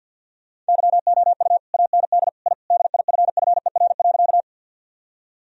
Morse code, heard as a keyed tone, about 49 wpm, 700 Hz. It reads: YOU AND I DIFFER.
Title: Sentences from Top 1000 Words - Review Part 1 - 50wpm